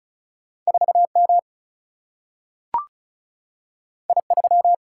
Morse code, heard as a keyed tone, about 36 wpm, 700 Hz. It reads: 4M I3
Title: Single-Letter Single-Number 35wpm